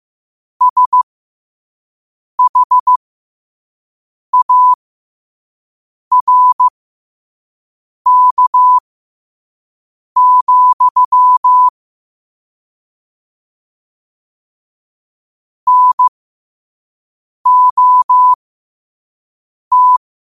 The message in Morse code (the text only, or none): SHARK, NOT